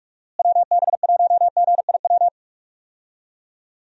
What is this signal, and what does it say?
Morse code, heard as a keyed tone, about 46 wpm, 700 Hz. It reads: WB1GIW
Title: USA Call Signs - 6 Characters - Repeated 3x - 45wpm